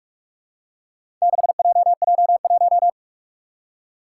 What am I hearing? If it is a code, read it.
Morse code, heard as a keyed tone, about 44 wpm, 700 Hz. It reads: 6JJ1
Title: Four Challenging Letter-Number Combinations 45wpm